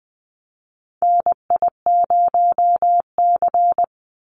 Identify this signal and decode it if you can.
Morse code, heard as a keyed tone, about 20 wpm, 700 Hz. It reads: NI0C
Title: Call Signs Repeated 2x - 20wpm